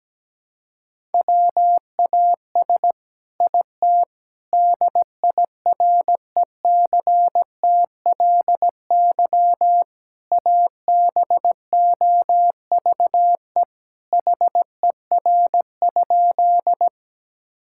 Morse code, heard as a keyed tone, about 17 wpm, 700 Hz. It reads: WAS IT DIRECTLY ABOVE HER?
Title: Sentences from Top 1000 Words - Review Part 5 - 17wpm